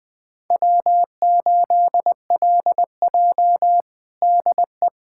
Morse code, heard as a keyed tone, about 20 wpm, 700 Hz. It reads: W8LJ DE
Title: USA Callsign DE Callsign - 4 Characters - 20wpm